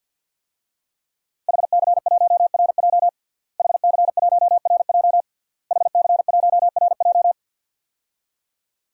Morse code, heard as a keyed tone, about 50 wpm, 700 Hz. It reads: HC1RJ HC1RJ HC1RJ
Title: DXCC-100 Call Signs - 5 Characters - Repeated 3x - 50wpm